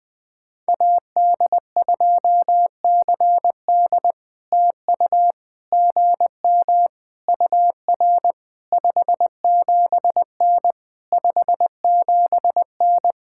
Morse code, heard as a keyed tone, about 20 wpm, 700 Hz. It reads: AD2CD TU GM UR 57N 57N
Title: POTA QSO Practice - 20wpm